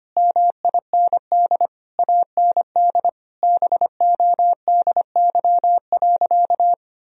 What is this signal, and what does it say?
Morse code, heard as a keyed tone, about 25 wpm, 700 Hz. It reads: MIND AND BODY.
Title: Sentences from Top 1000 Words - Review Part 2 - 25wpm